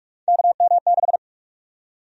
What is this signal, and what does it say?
Morse code, heard as a keyed tone, about 46 wpm, 700 Hz. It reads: KM6
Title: USA Call Sign Prefixes - 45wpm